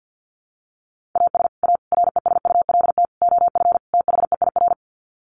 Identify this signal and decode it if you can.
Morse code, heard as a keyed tone, about 49 wpm, 700 Hz. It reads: AS A RESULT OF THEIR
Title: Sets of 5 Words Encore2 50wpm